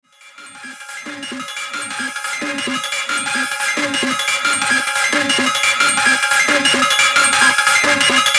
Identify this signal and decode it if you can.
Emergency Alert System (SAME) tones (header bursts), heard as harsh buzzing bursts. It is heard from 0.5 to 8.0 s.